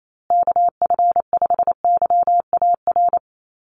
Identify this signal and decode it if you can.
Morse code, heard as a keyed tone, about 28 wpm, 700 Hz. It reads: KF5YAR
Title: USA Call Signs Repeated 3x 28wpm